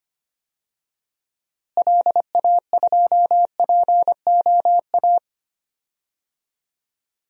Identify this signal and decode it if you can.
Morse code, heard as a keyed tone, about 25 wpm, 700 Hz. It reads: LA2POA
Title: DXCC-100 Call Signs - 6 Characters - Repeated 2x - 25wpm